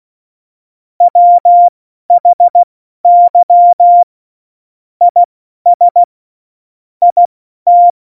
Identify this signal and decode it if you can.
Morse code, heard as a keyed tone, about 16 wpm, spaced as for 12 wpm, 700 Hz. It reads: WHY IS IT
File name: Morsekurs / Witze